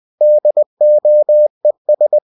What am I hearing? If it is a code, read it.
Morse code, heard as a keyed tone, about 20 wpm, 600 Hz. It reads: DOES
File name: cw-books / winnie-the-pooh / winnie-20wpm-page-0029